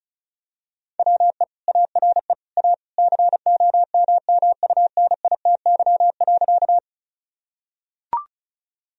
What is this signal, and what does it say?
Morse code, heard as a keyed tone, about 36 wpm, 700 Hz. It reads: WE ARE A COMMUNITY.
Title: Sentences from Top 1000 Words - Review Part 4 - 35wpm